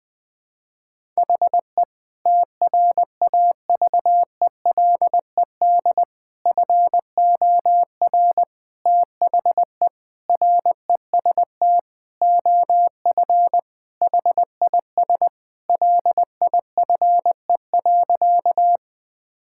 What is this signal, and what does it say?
Morse code, heard as a keyed tone, about 20 wpm, 700 Hz. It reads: HE TRAVELED FOR THE REST OF HIS LIFE.
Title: Sentences with 5-Word Phrases - 20wpm